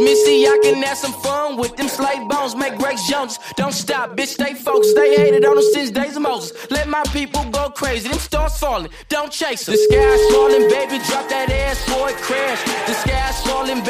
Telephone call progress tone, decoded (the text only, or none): ringback tone